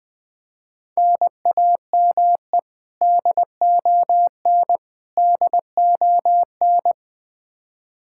Morse code, heard as a keyed tone, about 20 wpm, 700 Hz. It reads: NAME DON DON
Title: Practice QSO - Exchange 1 - Beginning - 20wpm